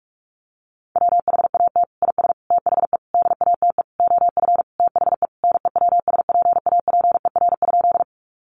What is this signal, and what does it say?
Morse code, heard as a keyed tone, about 44 wpm, 700 Hz. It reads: WHAT IS THE DATE OF THE NEWSPAPER?
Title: Sentences from Top 1100 Words 45wpm